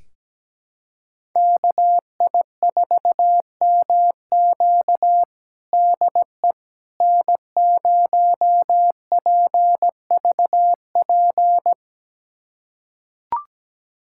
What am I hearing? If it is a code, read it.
Morse code, heard as a keyed tone, about 17 wpm, 700 Hz. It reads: KI4MQ DE N0PVP
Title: USA Callsign DE Callsign - 5 Characters - 17wpm